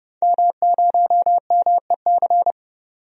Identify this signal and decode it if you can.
Morse code, heard as a keyed tone, about 30 wpm, 700 Hz. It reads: M0MEC
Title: DXCC-100 Call Signs - 5 Characters - 30wpm